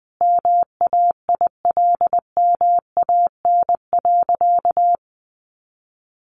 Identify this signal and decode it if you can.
Morse code, heard as a keyed tone, about 20 wpm, 700 Hz. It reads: MAILMAN.
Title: Additional Easy Phrases 20wpm